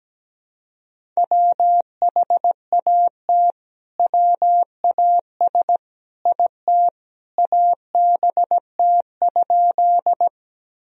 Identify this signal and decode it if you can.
Morse code, heard as a keyed tone, about 17 wpm, 700 Hz. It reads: WHAT WAS IT ABT?
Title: Sentences from Top 100 Words and Abbreviations 17wpm